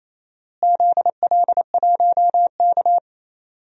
Morse code, heard as a keyed tone, about 28 wpm, 700 Hz. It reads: ZL1K
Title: DXCC-100 Call Signs - 4 Characters - Repeated 2x - 28wpm